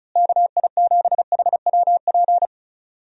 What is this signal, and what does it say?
Morse code, heard as a keyed tone, about 34 wpm, 700 Hz. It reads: KI7HWP
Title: USA Call Signs - 6 Characters - Repeated 3x - 35wpm